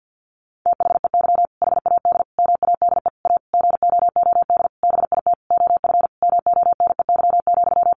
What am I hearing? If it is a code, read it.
Morse code, heard as a keyed tone, about 49 wpm, 700 Hz. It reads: THEY HAD MADE A GOOD BIT OF MONEY,
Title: Sentences from Top 900 Words 50wpm